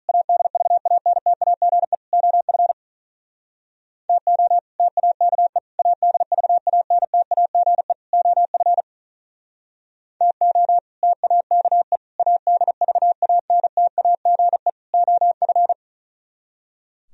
Morse code, heard as a keyed tone, about 45 wpm, 700 Hz. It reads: ADVANTAGE OF TO TAKE ADVANTAGE OF TO TAKE ADVANTAGE OF